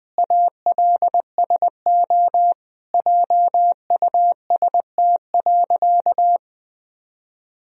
Morse code, heard as a keyed tone, about 20 wpm, 700 Hz. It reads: ALSO JUST.
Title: Sentences from Top 1000 Words - Review Part 1 - 20wpm